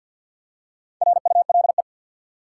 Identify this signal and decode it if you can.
Morse code, heard as a keyed tone, about 52 wpm, 700 Hz. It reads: RULE
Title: Four Letter Words - 50wpm